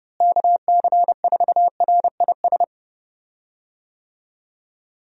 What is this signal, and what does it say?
Morse code, heard as a keyed tone, about 32 wpm, 700 Hz. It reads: KC4RIS
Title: USA Call Signs - 6 Characters - Repeated 3x - 30wpm